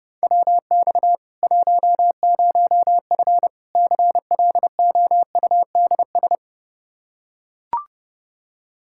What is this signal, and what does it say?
Morse code, heard as a keyed tone, about 30 wpm, 700 Hz. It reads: WX 10F CLOUDS
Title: Practice QSO - Exchange 2 - Info - 30wpm